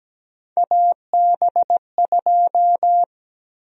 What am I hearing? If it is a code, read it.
Morse code, heard as a keyed tone, about 17 wpm, 700 Hz. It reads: AB2
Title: USA Call Sign Prefixes - Repeated 2x - 17wpm